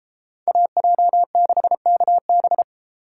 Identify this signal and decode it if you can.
Morse code, heard as a keyed tone, about 33 wpm, 700 Hz. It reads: AJ6KB